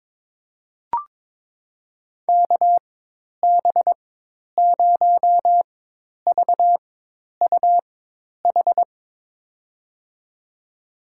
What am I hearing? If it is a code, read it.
Morse code, heard as a keyed tone, about 22 wpm, spaced as for 10 wpm, 700 Hz. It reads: KB0VUH
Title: USA Call Signs - 6 Characters - 4x Character Spacing - 22wpm